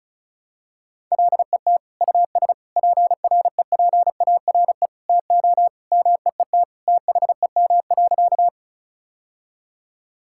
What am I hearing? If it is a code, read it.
Morse code, heard as a keyed tone, about 36 wpm, 700 Hz. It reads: LET US PREPARE TO MEET THEM.
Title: Sentences from Top 1000 Words - Review Part 6 - 35wpm